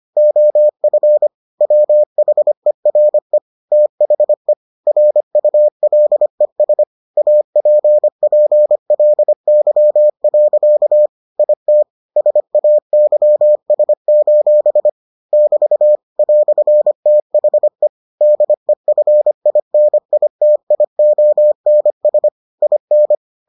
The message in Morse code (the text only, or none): OF WHERE THE RULES APPLY. IT SAYS: <BT> "THE DEFINITIONS IN